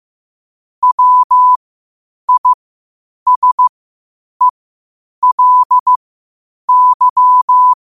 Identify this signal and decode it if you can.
Morse code, heard as a keyed tone, about 15 wpm, spaced as for 8 wpm, 1000 Hz. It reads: WISELY
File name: qotdfiles / 8wpm